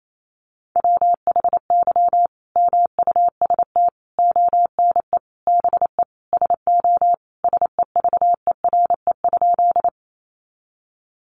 Morse code, heard as a keyed tone, about 28 wpm, 700 Hz. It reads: WHY MUST ONE BE SO SEVERE?
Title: Sentences from Top 1400 Words 28wpm